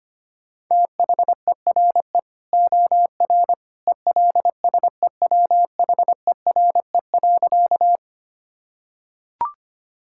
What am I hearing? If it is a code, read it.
Morse code, heard as a keyed tone, about 25 wpm, 700 Hz. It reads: THERE OR ELSEWHERE.
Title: Sentences from Top 1600 Words 25wpm